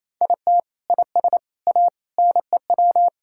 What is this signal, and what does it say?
Morse code, heard as a keyed tone, about 28 wpm, 700 Hz. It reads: IT IS A NEW
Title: Sentences from Top 1000 Words - Review Part 6 - 28wpm